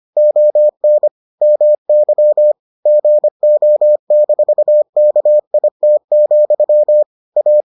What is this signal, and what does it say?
Morse code, heard as a keyed tone, about 25 wpm, 600 Hz. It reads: ON MY GO-KIT, A